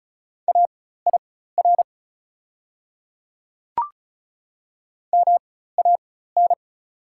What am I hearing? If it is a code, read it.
Morse code, heard as a keyed tone, about 36 wpm, spaced as for 16 wpm, 690 Hz. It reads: AIR MAN
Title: 4x Character Spacing - Top 200 Words - 35wpm